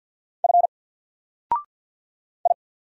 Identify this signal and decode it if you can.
Morse code, heard as a keyed tone, about 48 wpm, 700 Hz. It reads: F I
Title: Single Character (Letter:Number:Punctuation:Prosign) Alt 50wpm